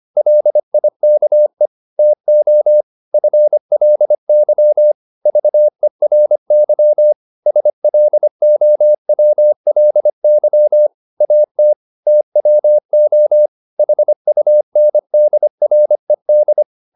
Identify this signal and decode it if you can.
Morse code, heard as a keyed tone, about 25 wpm, 600 Hz. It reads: LIKE TO FLY VERY SLOWLY AT TWO HUNDRED